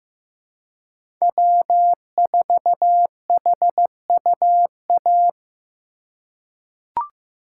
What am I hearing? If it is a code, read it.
Morse code, heard as a keyed tone, about 15 wpm, 700 Hz. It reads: W4HUA
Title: USA Call Signs - 5 Characters - 4x Character Spacing - 15wpm